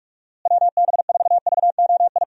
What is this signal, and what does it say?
Morse code, heard as a keyed tone, about 41 wpm, 700 Hz. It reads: WB4VOI
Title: USA Call Signs - 6 Characters - Repeated 3x - 45wpm